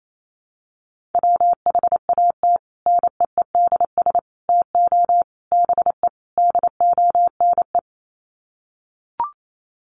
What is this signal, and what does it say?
Morse code, heard as a keyed tone, about 28 wpm, 700 Hz. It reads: WHAT NEEDS TO BE DONE
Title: Sets of 5 Words Encore 28wpm